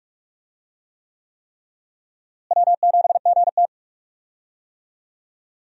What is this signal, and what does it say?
Morse code, heard as a keyed tone, about 41 wpm, 700 Hz. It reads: W7GT
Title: USA Call Signs - 4 Characters - Repeated 3x - 45wpm